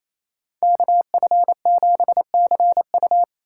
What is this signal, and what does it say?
Morse code, heard as a keyed tone, about 28 wpm, 700 Hz. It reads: KF7CU
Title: USA Call Signs - 4x Character Spacing - 28wpm